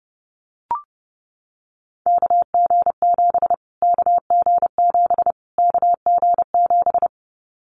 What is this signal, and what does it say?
Morse code, heard as a keyed tone, about 30 wpm, 700 Hz. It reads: KG7 KG7 KG7
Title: Callsign Prefixes for Top 25 DXCC - Repeated 3x - 30wpm